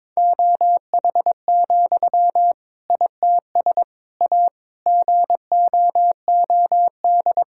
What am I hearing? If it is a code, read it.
Morse code, heard as a keyed tone, about 22 wpm, 700 Hz. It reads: OH, ITS A GOOD
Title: Sentences from Top 1000 Words - Review Part 5 - 22wpm